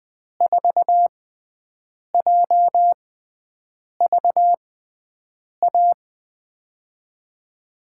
Morse code, heard as a keyed tone, about 20 wpm, spaced as for 7 wpm, 700 Hz. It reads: 4JVA
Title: USA Call Signs - 5 Characters - 6x Character Spacing - 20wpm